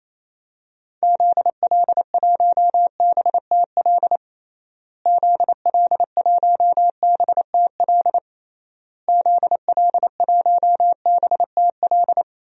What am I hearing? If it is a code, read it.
Morse code, heard as a keyed tone, about 28 wpm, 700 Hz. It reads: ZL1BTL ZL1BTL ZL1BTL
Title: DXCC-100 Call Signs - 6 Characters - Repeated 3x - 28wpm